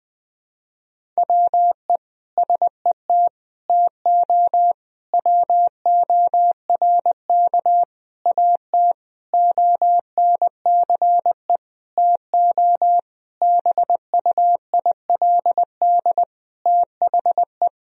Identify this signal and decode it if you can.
Morse code, heard as a keyed tone, about 20 wpm, 700 Hz. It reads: WE SET TO WORK AT ONCE TO BUILD THE
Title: Sentences from Top 1000 Words - Review Part 1 - 20wpm